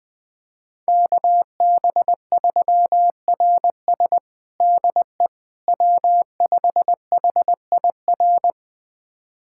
Morse code, heard as a keyed tone, about 21 wpm, 700 Hz. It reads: KB3RS DE W5HIR
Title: USA Callsign DE Callsign 20wpm